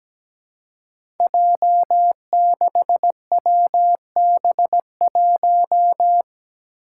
Morse code, heard as a keyed tone, about 17 wpm, 700 Hz. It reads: J6WB1
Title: Five Challenging Letter-Number Combinations 17wpm